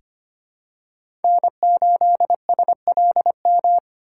Morse code, heard as a keyed tone, about 25 wpm, 700 Hz. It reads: N8SLM
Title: USA Call Signs - 5 Characters - 3x Character Spacing - 25wpm